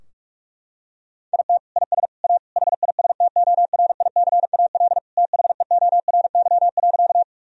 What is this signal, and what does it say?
Morse code, heard as a keyed tone, about 42 wpm, 700 Hz. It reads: IT IS A HISTORICAL THEORY.
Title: Sentences from Top 1000 Words - Review Part 4 - 45wpm